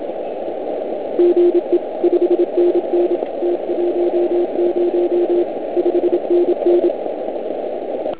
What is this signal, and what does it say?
Morse code, heard as a keyed tone, about 26 wpm, 350 Hz. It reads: GE 5NN T10 5NN